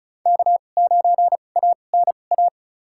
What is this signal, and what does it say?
Morse code, heard as a keyed tone, about 34 wpm, spaced as for 25 wpm, 700 Hz. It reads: K9ANA
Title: USA Call Signs - 5 Characters - 2x Character Spacing - 35wpm